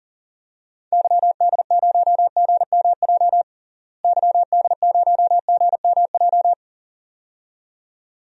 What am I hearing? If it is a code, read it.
Morse code, heard as a keyed tone, about 40 wpm, 700 Hz. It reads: YD0GMJ YD0GMJ
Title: DXCC-100 Call Signs - 6 Characters - Repeated 2x - 40wpm